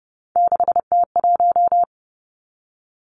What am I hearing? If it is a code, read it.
Morse code, heard as a keyed tone, about 30 wpm, 700 Hz. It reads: 6T1
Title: Three Challenging Letter-Number Combinations 30wpm